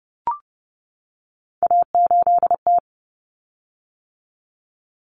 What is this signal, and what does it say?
Morse code, heard as a keyed tone, about 30 wpm, 700 Hz. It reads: A8T
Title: Common Cut Numbers - 3-Digit 30wpm